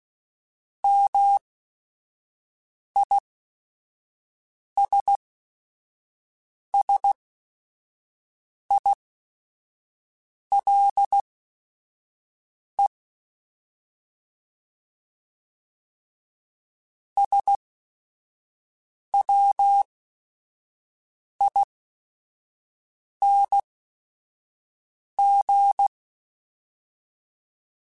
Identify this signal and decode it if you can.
Morse code, heard as a keyed tone, about 16 wpm, spaced as for 5 wpm, 800 Hz. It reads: MISSILE SWING
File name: Short Runs / Short Runs-mp3s / Words @5